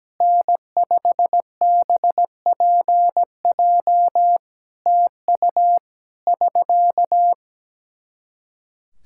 Morse code, heard as a keyed tone, about 17 wpm, 700 Hz. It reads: N5BPJ TU <SK>
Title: Practice QSO - Exchange 4 - Ending - 17wpm